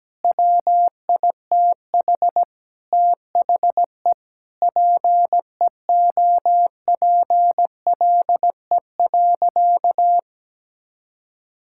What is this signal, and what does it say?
Morse code, heard as a keyed tone, about 17 wpm, 700 Hz. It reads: WITH THE PEOPLE.
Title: Sentences from Top 1000 Words - Review Part 5 - 17wpm